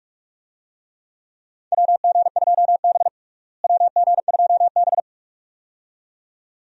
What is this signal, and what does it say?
Morse code, heard as a keyed tone, about 46 wpm, 700 Hz. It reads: WG2B WG2B
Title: USA Call Signs - 4 Characters - Repeated 2x - 45wpm